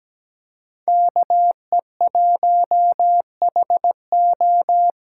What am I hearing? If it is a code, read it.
Morse code, heard as a keyed tone, about 17 wpm, 700 Hz. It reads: KE1HO
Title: USA Call Signs - 5 Characters - 3x Character Spacing - 17wpm